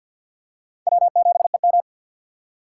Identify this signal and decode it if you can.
Morse code, heard as a keyed tone, about 52 wpm, 700 Hz. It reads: W7EM
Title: USA Call Signs - 4 Characters - 6x Character Spacing - 50wpm